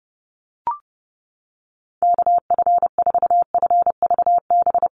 Morse code, heard as a keyed tone, about 30 wpm, 700 Hz. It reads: KF4FVB
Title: USA Call Signs Repeated 3x 30wpm